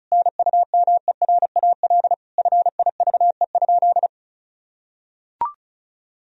Morse code, heard as a keyed tone, about 36 wpm, 700 Hz. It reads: NUMERAL FIVE?
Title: Sentences from Top 1000 Words - Review Part 2 - 35wpm